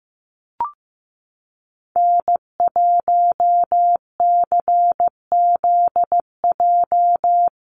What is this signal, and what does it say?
Morse code, heard as a keyed tone, about 15 wpm, 700 Hz. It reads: N1CZJ
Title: USA Call Signs Repeated 3x 15wpm